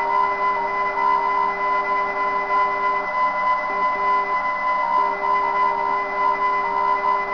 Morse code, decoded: none